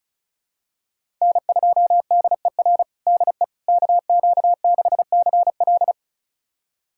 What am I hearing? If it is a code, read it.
Morse code, heard as a keyed tone, about 36 wpm, 690 Hz. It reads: N2DER DE KQ6CL
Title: USA Callsign DE Callsign - 5 Characters - 35wpm